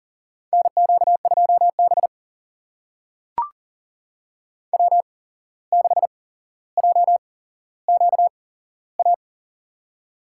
Morse code, heard as a keyed tone, about 40 wpm, spaced as for 16 wpm, 700 Hz. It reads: NQ2B W6JQA
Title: USA Call Signs - 8x Character Spacing - 40wpm